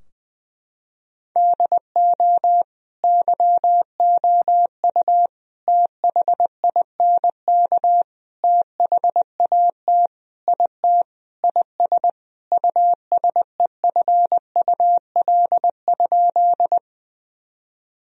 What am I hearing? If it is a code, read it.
Morse code, heard as a keyed tone, about 20 wpm, 700 Hz. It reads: DO YOU THINK THAT IT IS USEFUL?
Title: Sentences from Top 1000 Words - Review Part 4 - 20wpm